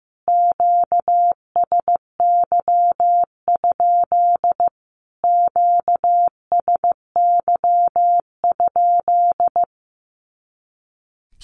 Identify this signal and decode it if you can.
Morse code, heard as a keyed tone, about 14 wpm, 700 Hz. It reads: QSY? QSY?
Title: Q-Codes Repeated 3x 15wpm